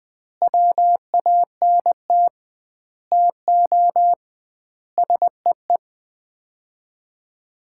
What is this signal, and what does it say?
Morse code, heard as a keyed tone, about 20 wpm, 700 Hz. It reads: WANT TO SEE
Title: 2x Word-Spacing - Sets of 5 Words - 20wpm